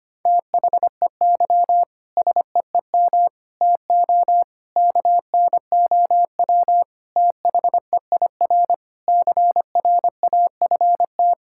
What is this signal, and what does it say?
Morse code, heard as a keyed tone, about 25 wpm, 700 Hz. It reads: THEY SEEM TO KNOW THEIR CRAFT
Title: Sentences from Top 1000 Words - Review Part 6 - 25wpm